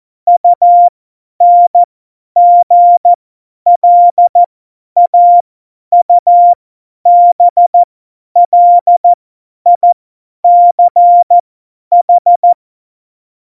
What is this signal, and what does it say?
Morse code, heard as a keyed tone, about 14 wpm, spaced as for 10 wpm, 700 Hz. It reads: UNGLAUBLICH
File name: Morsekurs / Witze / Witz 03.Dezember 2019, 50 F70